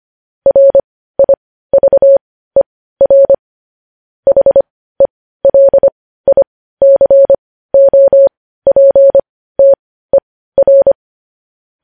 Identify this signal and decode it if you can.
Morse code, heard as a keyed tone, about 25 wpm, spaced as for 15 wpm, 550 Hz. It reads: RIVER HELICOPTER
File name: podcast / News headlines-1x-25x15